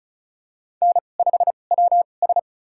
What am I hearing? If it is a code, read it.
Morse code, heard as a keyed tone, about 36 wpm, spaced as for 25 wpm, 700 Hz. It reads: N5WS
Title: USA Call Signs - 4 Characters - 2x Character Spacing - 35wpm